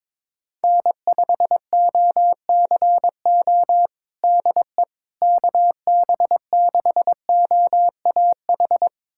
Morse code, heard as a keyed tone, about 22 wpm, 700 Hz. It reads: N5OCO DE KB6OAH
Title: USA Callsign DE Callsign 22wpm